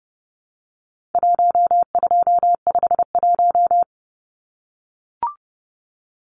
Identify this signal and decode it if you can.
Morse code, heard as a keyed tone, about 30 wpm, 700 Hz. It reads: 1251
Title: Numbers - Contest Style - 4-Digit 30wpm